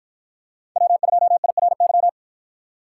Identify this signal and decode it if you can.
Morse code, heard as a keyed tone, about 53 wpm, 700 Hz. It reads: W2IRY